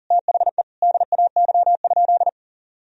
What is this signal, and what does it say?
Morse code, heard as a keyed tone, about 40 wpm, 700 Hz. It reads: THE DAY?
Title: Sentences from Top 1000 Words - Review Part 4 - 40wpm